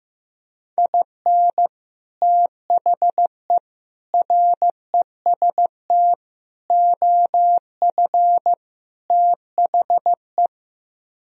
Morse code, heard as a keyed tone, about 15 wpm, 700 Hz. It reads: IN THE REST OF THE
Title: Speed-Racing - Sets of 5 Words 15wpm